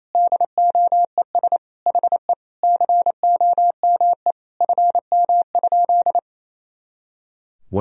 Morse code, heard as a keyed tone, about 28 wpm, 700 Hz. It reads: DOES HE COME FM?
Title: Sentences from Top 300 Words and Abbreviations 28wpm